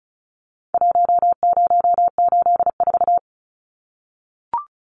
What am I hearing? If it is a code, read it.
Morse code, heard as a keyed tone, about 34 wpm, 700 Hz. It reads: 1084
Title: Numbers - Contest Style - 4-Digit 35wpm